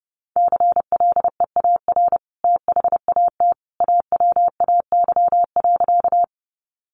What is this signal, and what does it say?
Morse code, heard as a keyed tone, about 30 wpm, 700 Hz. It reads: CLEAR THAT AWAY.
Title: Sentences from Top 600 Words 30wpm